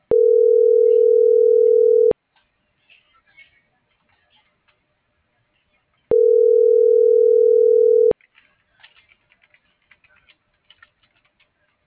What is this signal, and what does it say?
Ringback tone (North America), 440 + 480 Hz.